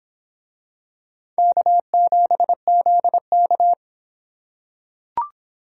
Morse code, heard as a keyed tone, about 26 wpm, 700 Hz. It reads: K7ZK